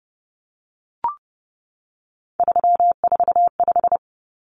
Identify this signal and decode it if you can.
Morse code, heard as a keyed tone, about 30 wpm, 700 Hz. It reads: 345
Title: Common Cut Numbers - 3-Digit 30wpm